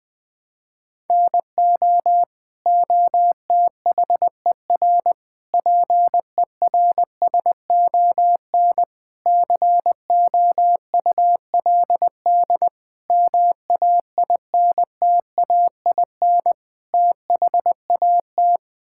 Morse code, heard as a keyed tone, about 21 wpm, 700 Hz. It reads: NO OTHER PERSON COULD MAINTAIN THAT
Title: Sentences from Top 1000 Words - Review Part 5 - 20wpm